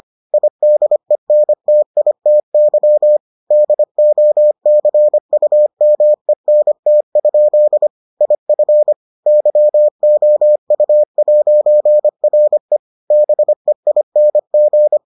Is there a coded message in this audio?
Yes: Morse code from 0.3 to 15.0 s, heard as a keyed tone, about 25 wpm, 600 Hz. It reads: IDENTITY DOCUMENT? IF YOU'RE BEING